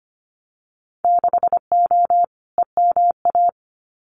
Morse code, heard as a keyed tone, about 25 wpm, 700 Hz. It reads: 6O EMA
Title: Winter Field Day Exchanges Encore 25wpm